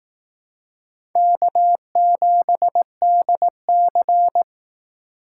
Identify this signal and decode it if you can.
Morse code, heard as a keyed tone, about 18 wpm, 700 Hz. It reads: K7DC
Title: Speed Racing - USA 1x2 Call Signs 15wpm